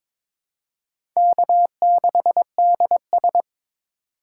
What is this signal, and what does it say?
Morse code, heard as a keyed tone, about 22 wpm, 700 Hz. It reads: K6DS
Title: USA Call Signs - 4 Characters - 4x Character Spacing - 22wpm